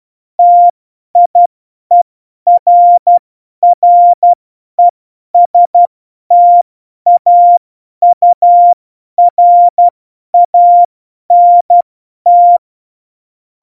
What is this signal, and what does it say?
Morse code, heard as a keyed tone, about 12 wpm, spaced as for 10 wpm, 700 Hz. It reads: TIERRESTAURANT
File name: Morsekurs / Witze / Witz 20. November 2018, 50 F70